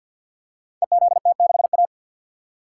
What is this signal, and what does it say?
Morse code, heard as a keyed tone, about 52 wpm, 700 Hz. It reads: EZT6A
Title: Code Groups - 5 Letter:Numbers - 50wpm